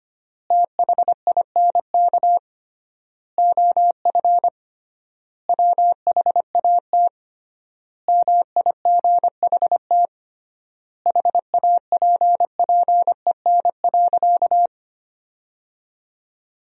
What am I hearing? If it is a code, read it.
Morse code, heard as a keyed tone, about 25 wpm, 700 Hz. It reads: THINK OF WHAT MIGHT HAPPEN.
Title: 3x Word-Spacing - Sentences from Top 400 Words - 25wpm